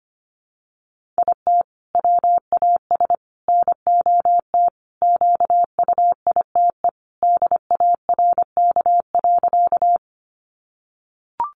Morse code, heard as a keyed tone, about 25 wpm, 700 Hz. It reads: IT WAS NOT QUITE DARK.
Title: Sentences from Top 700 Words 25wpm